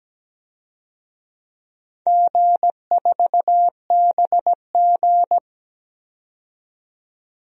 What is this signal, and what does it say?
Morse code, heard as a keyed tone, about 17 wpm, 700 Hz. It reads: G4BG
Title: DXCC-100 Call Signs - 4 Characters - 17wpm